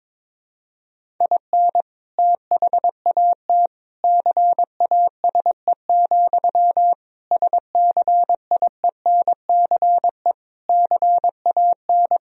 Morse code, heard as a keyed tone, about 22 wpm, 700 Hz. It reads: IN THAT CASE, SCIENCE CAN
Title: Sentences from Top 1000 Words - Review Part 2 - 22wpm